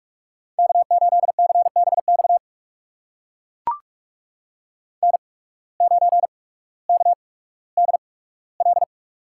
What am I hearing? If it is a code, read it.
Morse code, heard as a keyed tone, about 42 wpm, spaced as for 19 wpm, 700 Hz. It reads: K8CBX N9KDL
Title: USA Call Signs - 8x Character Spacing - 45wpm